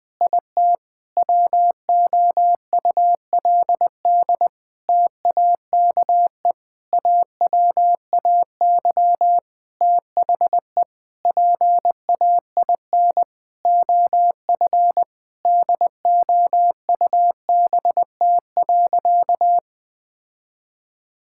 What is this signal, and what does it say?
Morse code, heard as a keyed tone, about 20 wpm, 700 Hz. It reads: IT WOULD TAKE AWAY THE PAIN OF DOUBT.
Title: Sentences from Top 1600 Words 20wpm